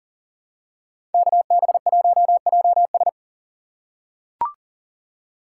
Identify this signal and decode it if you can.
Morse code, heard as a keyed tone, about 40 wpm, 700 Hz. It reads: KB1JS
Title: USA Call Signs - 5 Characters - 2x Character Spacing - 40wpm